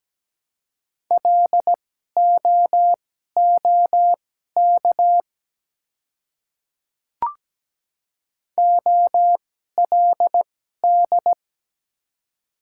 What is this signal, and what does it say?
Morse code, heard as a keyed tone, about 17 wpm, spaced as for 12 wpm, 700 Hz. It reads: LOOK OLD
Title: 2x Character Spacing - Top 200 Words - 17wpm